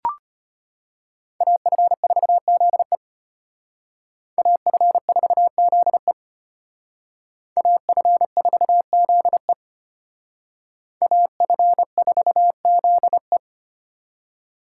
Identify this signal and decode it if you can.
Morse code, heard as a keyed tone, about 33 wpm, 700 Hz. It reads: AF4ZE AF4ZE AF4ZE AF4ZE